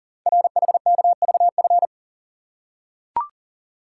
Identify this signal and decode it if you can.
Morse code, heard as a keyed tone, about 43 wpm, 700 Hz. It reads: RHKVF
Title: Code Groups - 5 Letters - 40wpm